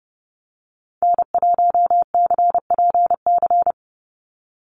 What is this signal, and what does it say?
Morse code, heard as a keyed tone, about 30 wpm, 700 Hz. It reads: N1CPC
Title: USA Call Signs 30wpm